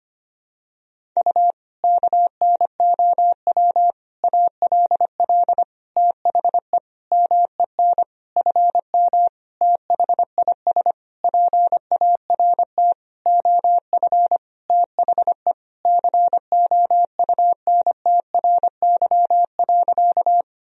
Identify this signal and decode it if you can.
Morse code, heard as a keyed tone, about 25 wpm, 700 Hz. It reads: U KNOW ALL THE MEN FM THIS PART OF THE COUNTRY.
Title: Sentences from Top 300 Words and Abbreviations 25wpm